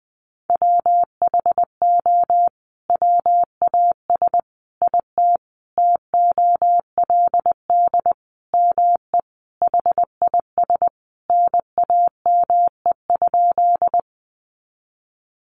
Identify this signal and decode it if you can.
Morse code, heard as a keyed tone, about 19 wpm, 700 Hz. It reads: WHO WAS IT TOLD ME HIS NAME?
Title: Sentences from Top 400 Words 20wpm